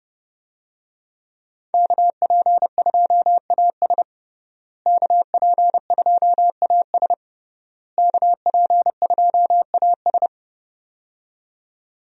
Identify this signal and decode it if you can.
Morse code, heard as a keyed tone, about 31 wpm, 700 Hz. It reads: KP2AS KP2AS KP2AS
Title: USA Call Signs - 5 Characters - Repeated 3x - 30wpm